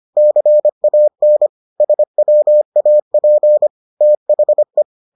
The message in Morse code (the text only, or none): CAN SWAP THE